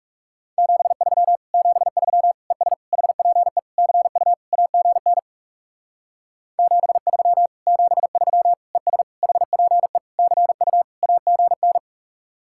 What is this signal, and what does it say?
Morse code, heard as a keyed tone, about 39 wpm, 700 Hz. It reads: 73 73 ES HPE CU AGN 73 73 ES HPE CU AGN